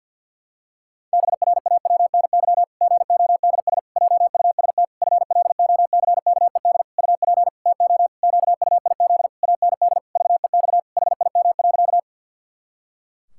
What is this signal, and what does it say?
Morse code, heard as a keyed tone, about 52 wpm, 700 Hz. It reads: BRAWNY GODS JUST FLOCKED UP TO QUIZ AND VEX HIM.
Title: Pangram Challenge v2 - 50wpm